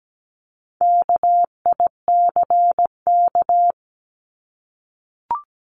Morse code, heard as a keyed tone, about 17 wpm, 700 Hz. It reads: KICK
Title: Top 1251-1500 Words 17wpm